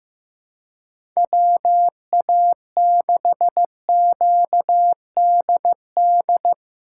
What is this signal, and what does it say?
Morse code, heard as a keyed tone, about 15 wpm, 700 Hz. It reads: WA6QDD
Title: USA Call Signs - 6 Characters - 6x Character Spacing - 15wpm